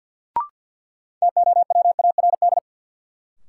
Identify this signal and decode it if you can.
Morse code, heard as a keyed tone, about 50 wpm, 700 Hz. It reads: TOWARD
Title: Sentences from Top 1300 Words - New Word Drill Alt - 50wpm